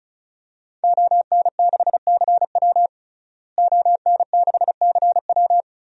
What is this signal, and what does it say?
Morse code, heard as a keyed tone, about 36 wpm, 700 Hz. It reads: ON6CW ON6CW
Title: DXCC-100 Call Signs - 5 Characters - Repeated 3x - 35wpm